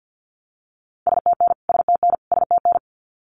Morse code, heard as a keyed tone, about 50 wpm, 700 Hz. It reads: STN STN STN
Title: Abbreviations Repeated 3x - 50wpm